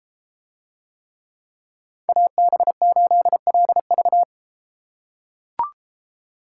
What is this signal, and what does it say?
Morse code, heard as a keyed tone, about 33 wpm, 700 Hz. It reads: AB8LV